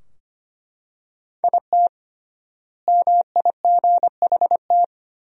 Morse code, heard as a keyed tone, about 25 wpm, 700 Hz. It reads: IT MIGHT
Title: 3x Word-Spacing - Sentences from Top 300 Words - 25wpm